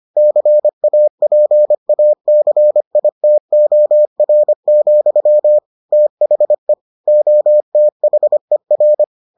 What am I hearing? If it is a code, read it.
Morse code, heard as a keyed tone, about 25 wpm, 600 Hz. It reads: CAPACITOR, THE OTHER